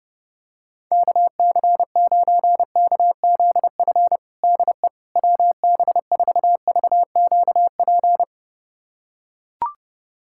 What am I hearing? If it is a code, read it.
Morse code, heard as a keyed tone, about 31 wpm, 700 Hz. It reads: KC9KZF DE WB4VQP
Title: USA Callsign DE Callsign - 6 Characters - 30wpm